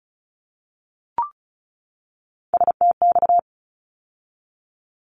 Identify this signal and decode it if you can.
Morse code, heard as a keyed tone, about 34 wpm, 700 Hz. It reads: STX
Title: ARRL Sections 35wpm